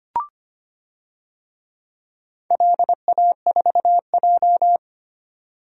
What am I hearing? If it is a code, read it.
Morse code, heard as a keyed tone, about 25 wpm, 700 Hz. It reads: LA4J
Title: DXCC-100 Call Signs - 4 Characters - 25wpm